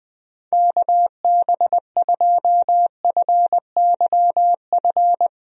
Morse code, heard as a keyed tone, about 21 wpm, 700 Hz. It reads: KB2FYF
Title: USA Call Signs - 6 Characters - Repeated 3x - 20wpm